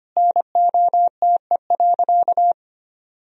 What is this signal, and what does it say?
Morse code, heard as a keyed tone, about 25 wpm, 700 Hz. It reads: NOTE.
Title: Sentences from Top 1000 Words - Review Part 2 - 25wpm